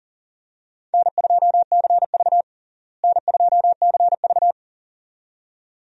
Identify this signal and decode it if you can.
Morse code, heard as a keyed tone, about 40 wpm, 700 Hz. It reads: N2CV N2CV
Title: USA Call Signs - 4 Characters - Repeated 2x - 40wpm